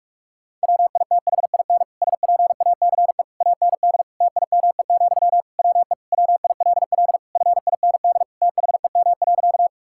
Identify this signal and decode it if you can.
Morse code, heard as a keyed tone, about 42 wpm, 700 Hz. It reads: WITHIN SPACE AND TIME, WE WILL FIND THEM.
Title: Sentences from Top 1000 Words - Review Part 5 - 45wpm